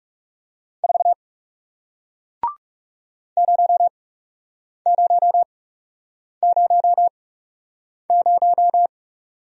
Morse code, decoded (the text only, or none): 4 0000